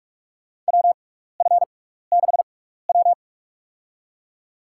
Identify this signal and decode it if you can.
Morse code, heard as a keyed tone, about 46 wpm, spaced as for 16 wpm, 700 Hz. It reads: WF6W
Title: USA Call Signs - 4 Characters - 6x Character Spacing - 45wpm